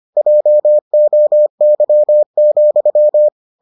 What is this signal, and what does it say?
Morse code, heard as a keyed tone, about 25 wpm, 600 Hz. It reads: JOY,